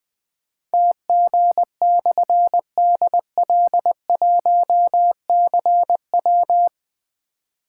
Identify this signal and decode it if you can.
Morse code, heard as a keyed tone, about 20 wpm, 700 Hz. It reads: TG/DL1CW
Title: Difficult Call Signs 20wpm